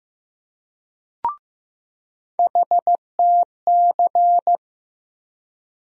Morse code, heard as a keyed tone, about 15 wpm, 700 Hz. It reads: HTC
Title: USA Call Sign Postfixes - 3 Letters 15wpm